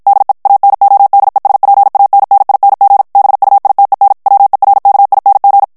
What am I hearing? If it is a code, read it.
Morse code, heard as a keyed tone, about 54 wpm, 800 Hz. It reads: DE ANODESPANNING BUITEN WERKING
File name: morse mp3 / HOLLANDAIS texte clair 2 vitesse 50 mots minute